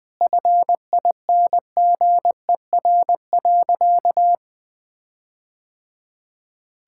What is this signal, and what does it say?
Morse code, heard as a keyed tone, about 20 wpm, 700 Hz. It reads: FINGER.
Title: Sentences from Top 1600 Words 20wpm